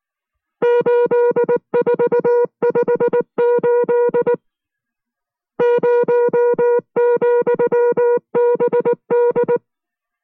Morse code, heard as a keyed tone, about 19 wpm, 480 Hz. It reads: 8458 0,BD